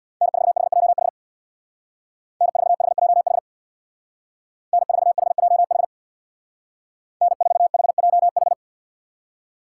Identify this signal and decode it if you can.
Morse code, heard as a keyed tone, about 60 wpm, 700 Hz. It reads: N4HJH N4HJH N4HJH N4HJH
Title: Speed Racing - USA 1x3 Call Signs 50wpm